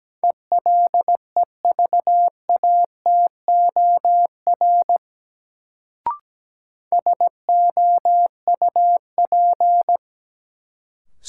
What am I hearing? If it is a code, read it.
Morse code, heard as a keyed tone, about 17 wpm, 700 Hz. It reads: ELEVATOR SOUP
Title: Sentences from Top 1600 Words - New Word Drill Alt - 17wpm